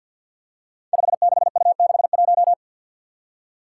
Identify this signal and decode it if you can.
Morse code, heard as a keyed tone, about 46 wpm, 700 Hz. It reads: 56U61
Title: Five Challenging Letter-Number Combinations 50wpm